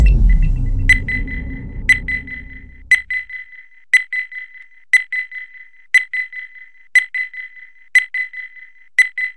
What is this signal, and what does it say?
There is no code.